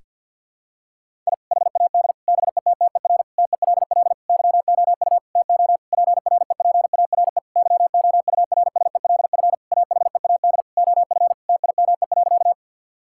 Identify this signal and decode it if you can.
Morse code, heard as a keyed tone, about 50 wpm, 700 Hz. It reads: I HAD BETTER TELL YOU TO PREPARE YOURSELF AHEAD OF TIME.
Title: Sentences from Top 1000 Words - Review Part 6 - 50wpm